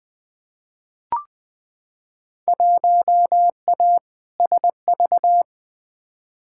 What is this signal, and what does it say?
Morse code, heard as a keyed tone, about 20 wpm, 700 Hz. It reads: 1A SV
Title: Field Day Exchanges Encore - 20wpm